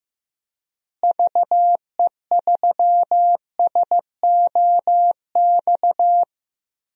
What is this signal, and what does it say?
Morse code, heard as a keyed tone, about 15 wpm, 700 Hz. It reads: VE3SOX
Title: DXCC-100 Call Signs - 6 Characters - Repeated 3x - 15wpm